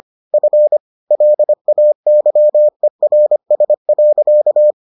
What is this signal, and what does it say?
Morse code, heard as a keyed tone, about 25 wpm, 600 Hz. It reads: F LAYERS.